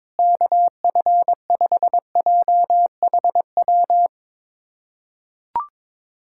Morse code, heard as a keyed tone, about 22 wpm, 700 Hz. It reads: KF5JHW
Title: USA Callsign DE Callsign 22wpm